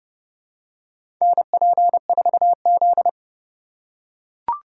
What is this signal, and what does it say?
Morse code, heard as a keyed tone, about 32 wpm, 700 Hz. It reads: NP4Z
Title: USA Call Signs - 4 Characters - 4x Character Spacing - 30wpm